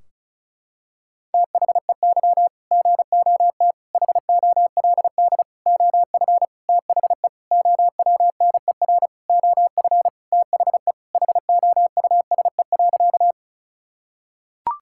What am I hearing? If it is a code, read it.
Morse code, heard as a keyed tone, about 36 wpm, 700 Hz. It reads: THEY GOT HOLD OF THE OWNER OF THE HOUSE.
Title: Sentences from Top 1600 Words 35wpm